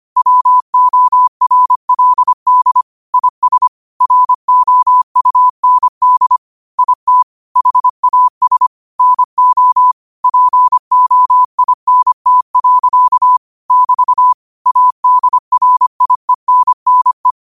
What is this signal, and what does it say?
Morse code, heard as a keyed tone, about 25 wpm, 1000 Hz. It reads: WORLD IS ROUND IT HAS NO POINT. <BT> ADRIENNE
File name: qotdfiles / 25wpm